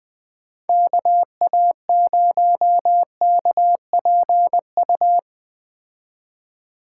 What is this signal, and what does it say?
Morse code, heard as a keyed tone, about 21 wpm, 700 Hz. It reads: KA0KPU
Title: Speed Racing - USA 2x3 Call Signs 17wpm